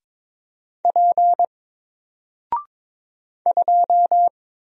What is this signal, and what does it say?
Morse code, heard as a keyed tone, about 22 wpm, 700 Hz. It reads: P 2
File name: Single-Letter-Number-22wpm